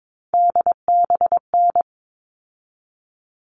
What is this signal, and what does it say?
Morse code, heard as a keyed tone, about 22 wpm, 700 Hz. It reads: DBN
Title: All Cut Numbers - 3-Digit 22wpm